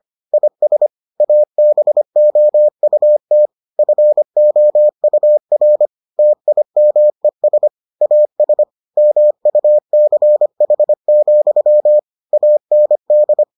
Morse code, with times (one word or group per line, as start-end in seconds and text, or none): 0.3-0.9 IS
1.2-3.5 ABOUT
3.8-5.9 FOUR
6.2-7.7 TIMES
8.0-8.6 AS
9.0-12.0 MUCH,
12.3-13.4 AND